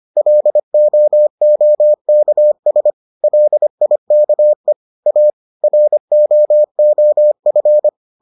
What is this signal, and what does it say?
Morse code, heard as a keyed tone, about 25 wpm, 600 Hz. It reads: LOOKS LIKE A ROOF